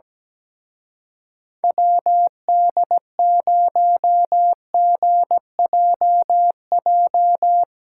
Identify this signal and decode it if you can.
Morse code, heard as a keyed tone, about 19 wpm, 700 Hz. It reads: WD0GJJ
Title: Speed Racing - USA 2x3 Call Signs 17wpm